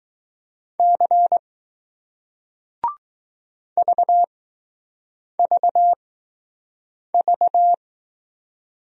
Morse code, heard as a keyed tone, about 20 wpm, 700 Hz. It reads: C V V V